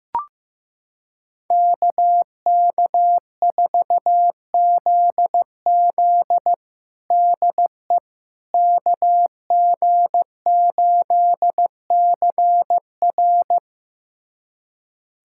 Morse code, heard as a keyed tone, about 15 wpm, 700 Hz. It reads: KK4ZZ DE KG8CR
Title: USA Callsign DE Callsign - 5 Characters - 15wpm